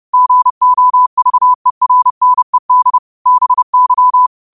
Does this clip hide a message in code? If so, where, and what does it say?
Yes: Morse code from 0.1 to 4.3 s, heard as a keyed tone, about 30 wpm, 1000 Hz. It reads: GOVERNED BY